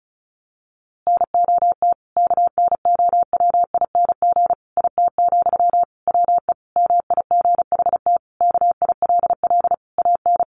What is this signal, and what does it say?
Morse code, heard as a keyed tone, about 34 wpm, 700 Hz. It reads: NOT KNOWING IT, WE MIGHT KILL AN
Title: A Sound Of Thunder 35wpm